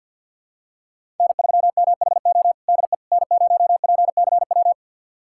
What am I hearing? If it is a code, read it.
Morse code, heard as a keyed tone, about 46 wpm, 700 Hz. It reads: N3MHO DE N0PCW
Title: USA Callsign DE Callsign - 5 Characters - 50wpm